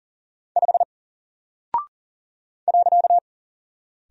Morse code, heard as a keyed tone, about 40 wpm, 700 Hz. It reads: 5 .
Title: Single Character (Letter:Number:Punctuation:Prosign) Alt 40wpm